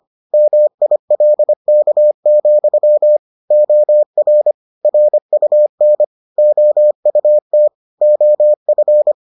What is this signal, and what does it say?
Morse code, heard as a keyed tone, about 25 wpm, 600 Hz. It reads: MILK, OR RUN OUT OF